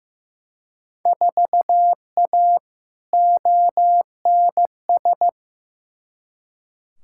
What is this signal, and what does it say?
Morse code, heard as a keyed tone, about 16 wpm, 700 Hz. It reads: 4A ONS
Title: Field Day Exchanges - 15wpm